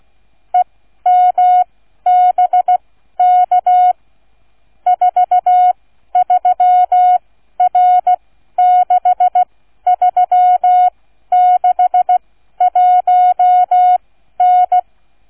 Morse code, heard as a keyed tone, about 16 wpm, spaced as for 12 wpm, 730 Hz. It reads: EMBK 43R6361N